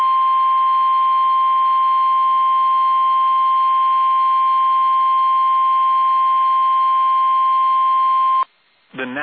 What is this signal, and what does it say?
Emergency Alert System (SAME) tones (the 1050 Hz attention tone), with speech. It is heard from 0.5 to 8.5 s.